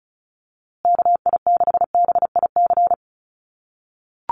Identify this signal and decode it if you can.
Morse code, heard as a keyed tone, about 34 wpm, 700 Hz. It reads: KI6BIC
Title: USA Call Signs Repeated 3x 35wpm